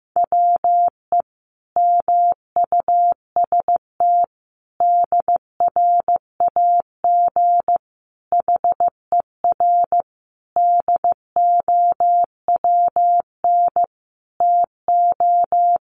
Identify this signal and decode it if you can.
Morse code, heard as a keyed tone, about 15 wpm, 700 Hz. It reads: WE MUST DRAG HER DOWN TO
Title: Sentences from Top 1500 Words 15wpm